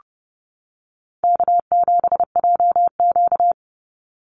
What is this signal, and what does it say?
Morse code, heard as a keyed tone, about 30 wpm, 700 Hz. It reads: K7JQ
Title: Easy Call Signs - 30wpm